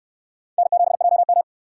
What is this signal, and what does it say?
Morse code, heard as a keyed tone, about 60 wpm, 700 Hz. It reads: N7JM